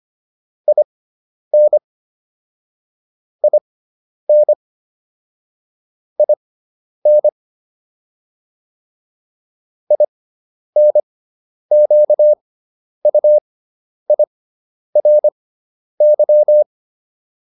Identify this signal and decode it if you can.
Morse code, heard as a keyed tone, about 25 wpm, spaced as for 10 wpm, 600 Hz. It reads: IN IN IN INQUIRY